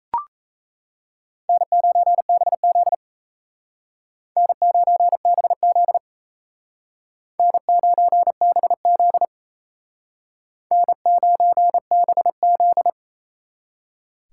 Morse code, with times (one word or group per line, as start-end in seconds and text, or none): 1.5-2.9 N9BZ
4.4-6.0 N9BZ
7.4-9.2 N9BZ
10.7-12.9 N9BZ